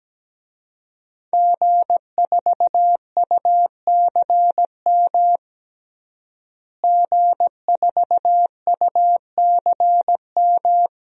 Morse code, heard as a keyed tone, about 17 wpm, 700 Hz. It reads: G4UCM G4UCM
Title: DXCC-100 Call Signs - 5 Characters - Repeated 2x - 17wpm